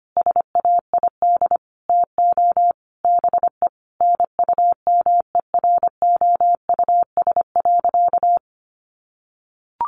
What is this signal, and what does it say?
Morse code, heard as a keyed tone, about 25 wpm, 700 Hz. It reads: SAID TO BE NUMEROUS.
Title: Sentences from Top 1400 Words 25wpm